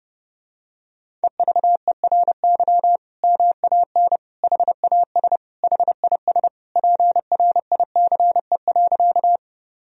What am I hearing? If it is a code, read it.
Morse code, heard as a keyed tone, about 32 wpm, 700 Hz. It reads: EVERY MAN HAS HIS PRICE.
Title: Sentences from Top 1000 Words - Review Part 4 - 30wpm